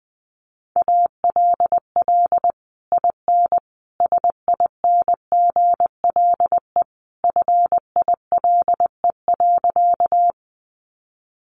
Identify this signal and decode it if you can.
Morse code, heard as a keyed tone, about 20 wpm, 700 Hz. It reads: ALL IN SINGLE FILE.
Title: Sentences from Top 1200 Words 20wpm